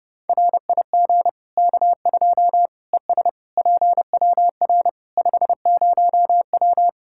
Morse code, heard as a keyed tone, about 32 wpm, 700 Hz. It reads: RIG K2 ES PWR 50W
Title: Practice QSO - Exchange 2 - Info - 30wpm